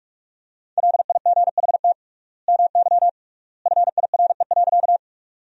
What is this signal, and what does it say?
Morse code, heard as a keyed tone, about 46 wpm, 700 Hz. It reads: LIGHT MY FIRE.
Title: 3x Word-Spacing - Sentences from Top 400 Words - 45wpm